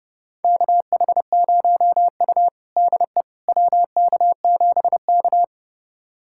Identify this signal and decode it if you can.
Morse code, heard as a keyed tone, about 29 wpm, 700 Hz. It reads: KH0U DE WK7K
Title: USA Callsign DE Callsign - 4 Characters - 30wpm